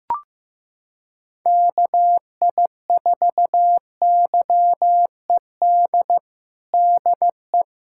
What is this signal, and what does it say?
Morse code, heard as a keyed tone, about 15 wpm, 700 Hz. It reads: KI4YED DE
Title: USA Callsign DE Callsign - 6 Characters - 15wpm